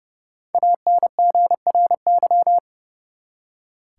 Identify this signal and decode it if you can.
Morse code, heard as a keyed tone, about 32 wpm, 700 Hz. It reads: ANGRY
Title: Five Letter Words - 30wpm